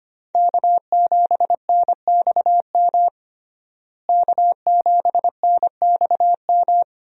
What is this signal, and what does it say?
Morse code, heard as a keyed tone, about 25 wpm, 700 Hz. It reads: K7NXM K7NXM
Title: USA Call Signs - 5 Characters - Repeated 2x - 25wpm